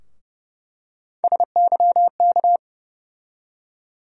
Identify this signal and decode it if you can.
Morse code, heard as a keyed tone, about 30 wpm, 700 Hz. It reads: SYK
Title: USA Call Sign Postfixes - 3 Letters 30wpm